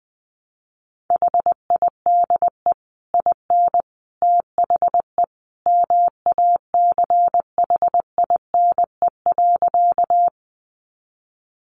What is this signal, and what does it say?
Morse code, heard as a keyed tone, about 20 wpm, 700 Hz. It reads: HIDE IN THE MACHINE.
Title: A Sound Of Thunder 20wpm